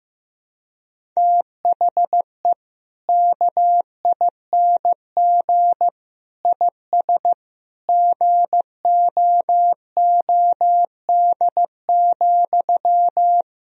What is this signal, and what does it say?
Morse code, heard as a keyed tone, about 15 wpm, 700 Hz. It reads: THE KING IS GOOD,
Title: Sentences from Top 1000 Words - Review Part 1 - 15wpm